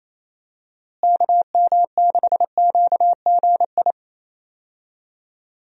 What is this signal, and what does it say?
Morse code, heard as a keyed tone, about 27 wpm, 700 Hz. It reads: KM6QGI
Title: USA Call Signs - 6 Characters - Repeated 2x - 28wpm